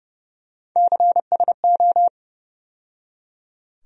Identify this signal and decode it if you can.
Morse code, heard as a keyed tone, about 30 wpm, 700 Hz. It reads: CSO
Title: USA Call Sign Postfixes - 3 Letters 30wpm